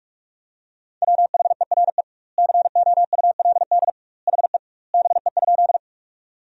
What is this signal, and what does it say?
Morse code, heard as a keyed tone, about 46 wpm, 700 Hz. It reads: WHERE COULD HE BE?
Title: 2x Word-Spacing - Sentences from Top 200 Words - 45wpm